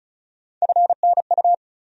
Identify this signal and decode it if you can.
Morse code, heard as a keyed tone, about 34 wpm, 700 Hz. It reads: FNU
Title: USA Call Sign Postfixes - 3 Letters 35wpm